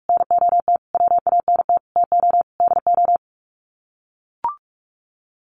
Morse code, heard as a keyed tone, about 44 wpm, 700 Hz. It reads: NOT WANT TO DO
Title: Sets of 5 Words 45wpm